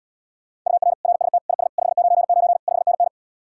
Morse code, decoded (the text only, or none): UR RST IS 599 5NN